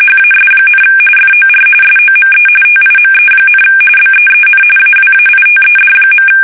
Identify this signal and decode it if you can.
RTTY, about 45.45 baud. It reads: HAS, FOR SOME TIME, BEEN AWAITED.